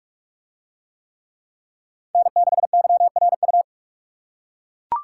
Morse code, heard as a keyed tone, about 40 wpm, 700 Hz. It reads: N6YRU